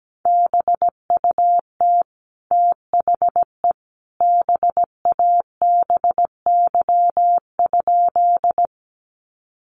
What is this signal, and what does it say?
Morse code, heard as a keyed tone, about 17 wpm, 700 Hz. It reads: BUT THE BABY?
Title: Sentences from Top 1300 Words 17wpm